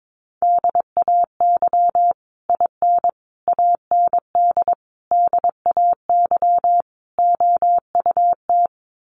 Morse code, heard as a keyed tone, about 22 wpm, 700 Hz. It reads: DAY IN AND DAY OUT
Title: Sets of 5 Words Encore2 22wpm